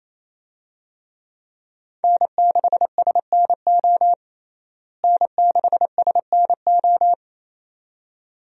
Morse code, heard as a keyed tone, about 28 wpm, 700 Hz. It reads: N6SNO N6SNO
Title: USA Call Signs - 5 Characters - Repeated 2x - 28wpm